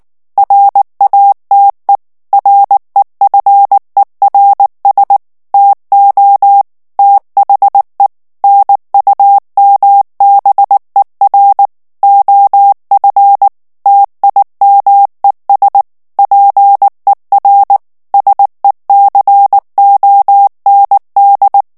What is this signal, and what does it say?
Morse code, heard as a keyed tone, about 19 wpm, 800 Hz. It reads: RATE REFERS TO THE NUMBER OF TIMES PER SECOND
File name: ANGLAIS texte clair 3 vitesse 18 mots minute